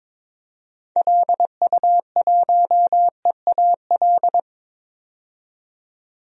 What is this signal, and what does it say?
Morse code, heard as a keyed tone, about 22 wpm, 700 Hz. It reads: LU1EAL
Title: DXCC-100 Call Signs - 6 Characters - 22wpm